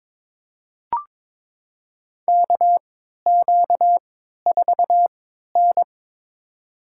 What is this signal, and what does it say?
Morse code, heard as a keyed tone, about 22 wpm, spaced as for 13 wpm, 700 Hz. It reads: KQ4N
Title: USA Call Signs - 3x Character Spacing - 22wpm